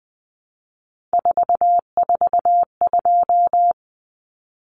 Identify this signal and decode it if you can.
Morse code, heard as a keyed tone, about 20 wpm, 700 Hz. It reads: 442
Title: Numbers - 3-Digit 20wpm